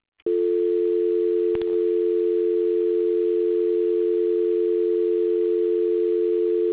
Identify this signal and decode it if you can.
Dial tone (North America), 350 + 440 Hz.
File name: things / telco facility trouble ESCLCAX0RS0